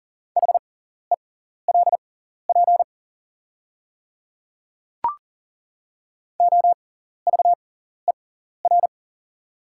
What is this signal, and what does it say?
Morse code, heard as a keyed tone, about 40 wpm, spaced as for 14 wpm, 700 Hz. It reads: HELP OVER
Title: 6x Character Spacing - Top 300 Words - 40wpm